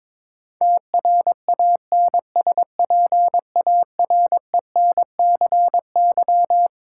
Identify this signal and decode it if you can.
Morse code, heard as a keyed tone, about 22 wpm, 700 Hz. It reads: TRANSPARENCY
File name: Twelve Letter Words - 22wpm